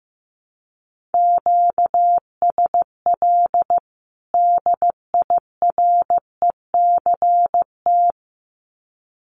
Morse code, heard as a keyed tone, about 15 wpm, 700 Hz. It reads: QSL DIRECT
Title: Common QSO Phrases 15wpm